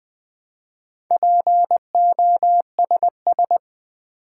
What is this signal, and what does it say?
Morse code, heard as a keyed tone, about 20 wpm, 700 Hz. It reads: POSS
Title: Build A Word - 2 Letters At A Time - 8-Letter Words - 20wpm